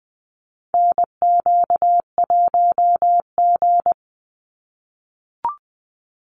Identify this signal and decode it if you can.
Morse code, heard as a keyed tone, about 20 wpm, 700 Hz. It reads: NQ1G
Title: USA Call Signs 20wpm